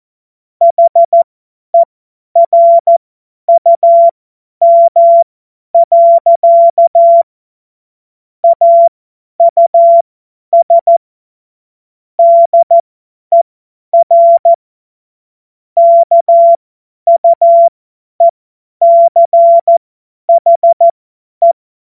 Morse code, heard as a keyed tone, about 14 wpm, spaced as for 10 wpm, 680 Hz. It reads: HERUM. AUS DER KUECHE
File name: Morsekurs / Witze / Witz 16. Februar 2021, 50 F70